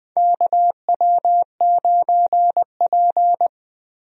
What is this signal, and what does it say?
Morse code, heard as a keyed tone, about 20 wpm, 700 Hz. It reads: KW9P
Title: USA Call Signs - 4 Characters - Repeated 3x - 20wpm